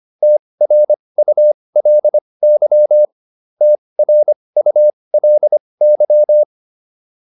Morse code, heard as a keyed tone, about 25 wpm, spaced as for 20 wpm, 600 Hz. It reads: TRULY TRULY